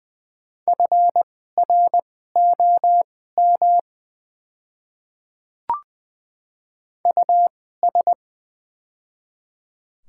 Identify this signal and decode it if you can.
Morse code, heard as a keyed tone, about 20 wpm, spaced as for 15 wpm, 690 Hz. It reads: FROM US
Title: 2x Character Spacing - Top 300 Words - 20wpm